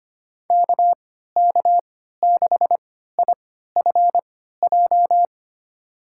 Morse code, heard as a keyed tone, about 25 wpm, spaced as for 14 wpm, 700 Hz. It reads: KK6IFJ
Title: USA Call Signs - 3x Character Spacing - 25wpm